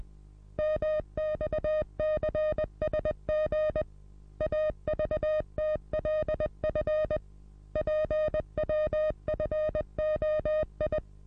Morse code, heard as a keyed tone, about 20 wpm, 620 Hz. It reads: MXCSG AVTLF PWFOI
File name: repos / Morsekurs / DJ4UF Morsen27